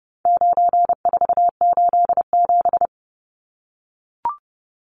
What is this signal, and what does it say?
Morse code, heard as a keyed tone, about 30 wpm, 700 Hz. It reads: 9487
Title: Numbers - 4-Digit 30wpm